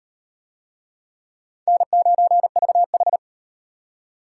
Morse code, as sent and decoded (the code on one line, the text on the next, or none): -. ----. ...- ....
N9VH